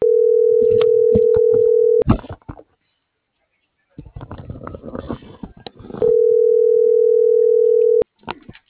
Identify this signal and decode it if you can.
Ringback tone (North America), 440 + 480 Hz.